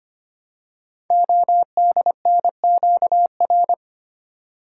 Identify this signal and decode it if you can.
Morse code, heard as a keyed tone, about 25 wpm, 700 Hz. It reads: ODNQR
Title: Code Groups - 5 Letters - 25wpm